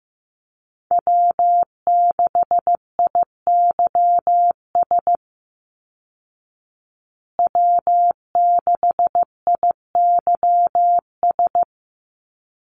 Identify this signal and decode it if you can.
Morse code, heard as a keyed tone, about 15 wpm, 700 Hz. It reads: W6IYS W6IYS
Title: Call Signs Repeated 3x - 15wpm